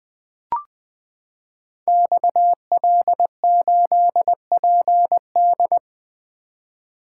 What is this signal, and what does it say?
Morse code, heard as a keyed tone, about 20 wpm, 700 Hz. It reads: XL8PD
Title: Code Groups - 5 Letter:Numbers - 20wpm